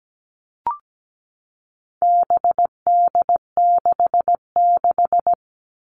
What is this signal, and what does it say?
Morse code, heard as a keyed tone, about 17 wpm, 700 Hz. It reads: BD66
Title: All Cut Numbers - 4-Digit 17wpm